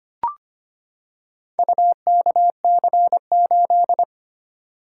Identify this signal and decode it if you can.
Morse code, heard as a keyed tone, about 25 wpm, 700 Hz. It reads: UKC8
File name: practice / 25wpm / Four Letter-Number Combinations - 25wpm